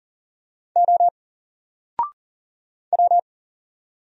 Morse code, heard as a keyed tone, about 40 wpm, 700 Hz. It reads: O W
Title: Single Letter-Number 40wpm